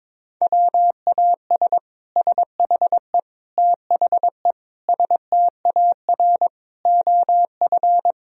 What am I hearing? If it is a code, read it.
Morse code, heard as a keyed tone, about 21 wpm, 700 Hz. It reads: WAS SHE THE STAR OF
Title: Sentences from Top 1000 Words - Review Part 3 - 22wpm